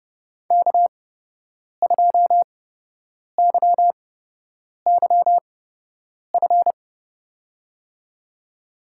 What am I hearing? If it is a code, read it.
Morse code, heard as a keyed tone, about 30 wpm, spaced as for 8 wpm, 700 Hz. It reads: K2YYF
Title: USA Call Signs - 8x Character Spacing - 30wpm